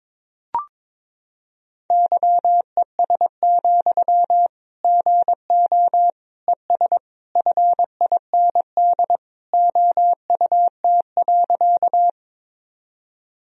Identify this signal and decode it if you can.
Morse code, heard as a keyed tone, about 22 wpm, 700 Hz. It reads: YES, GO ES FIND OUT.
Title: Sentences from Top 500 Words and Abbreviations 22wpm